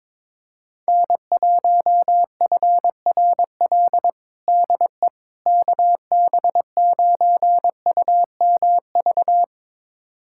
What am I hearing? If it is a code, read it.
Morse code, heard as a keyed tone, about 21 wpm, 700 Hz. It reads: N1FRL DE KB9UMV
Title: USA Callsign DE Callsign 22wpm